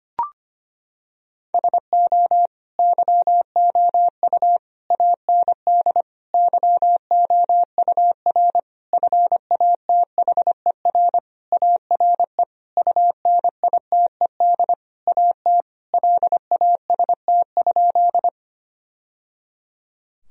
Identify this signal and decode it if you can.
Morse code, heard as a keyed tone, about 25 wpm, 700 Hz. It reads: SO YOU AND YOUR FATHER ARE UNITED AT LAST?
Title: Sentences from Top 1000 Words - Review Part 4 - 25wpm